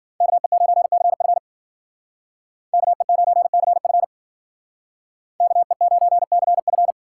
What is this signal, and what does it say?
Morse code, heard as a keyed tone, about 52 wpm, 700 Hz. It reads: KE9CF KE9CF KE9CF